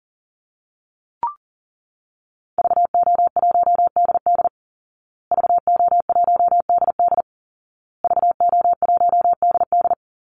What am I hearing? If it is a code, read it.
Morse code, heard as a keyed tone, about 40 wpm, 700 Hz. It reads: VO1DD VO1DD VO1DD
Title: Call Signs Repeated 2x - 40wpm